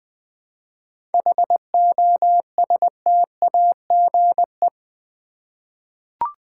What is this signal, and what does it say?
Morse code, heard as a keyed tone, about 20 wpm, 700 Hz. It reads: HOSTAGE
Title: 7 Letter Words - First 3 Letters Given - 20wpm